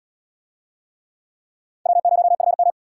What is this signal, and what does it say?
Morse code, heard as a keyed tone, about 77 wpm, 700 Hz. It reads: W0FM